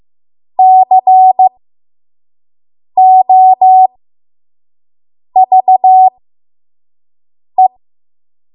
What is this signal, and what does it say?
Morse code, heard as a keyed tone, about 15 wpm, spaced as for 5 wpm, 750 Hz. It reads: COVE
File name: Archive / 5 WPM / 05WPM